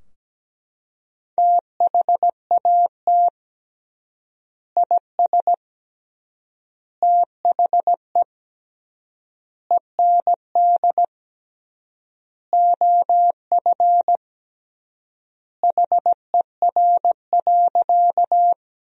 Morse code, heard as a keyed tone, about 17 wpm, 700 Hz. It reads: THAT IS THE END OF HER.
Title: 3x Word-Spacing - Sentences from Top 200 Words - 17wpm